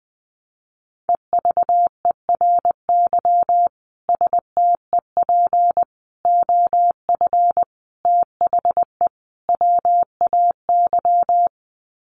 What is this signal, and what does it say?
Morse code, heard as a keyed tone, about 20 wpm, 700 Hz. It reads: EVERY STEP OF THE WAY
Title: Sets of 5 Words Encore2 20wpm